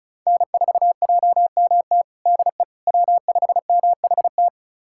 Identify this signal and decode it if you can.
Morse code, heard as a keyed tone, about 34 wpm, 700 Hz. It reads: N4JMT DE W5MHT
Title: USA Callsign DE Callsign - 5 Characters - 35wpm